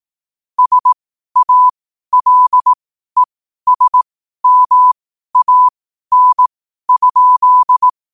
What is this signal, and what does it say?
Morse code, heard as a keyed tone, about 18 wpm, spaced as for 12 wpm, 1000 Hz. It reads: SALESMAN?
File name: qotdfiles / 12wpm